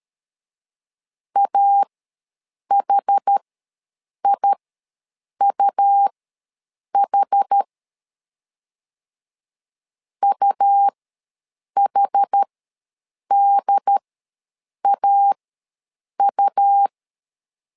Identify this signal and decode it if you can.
Morse code, heard as a keyed tone, about 13 wpm, spaced as for 7 wpm, 800 Hz. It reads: AHIUH UHDAU
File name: LEZIONI / 2 UHDN / 2 UHD ws 12wpm 1722